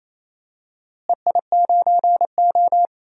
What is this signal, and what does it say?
Morse code, heard as a keyed tone, about 28 wpm, 700 Hz. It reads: EI9O
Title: DXCC-100 Call Signs - 4 Characters - Repeated 3x - 28wpm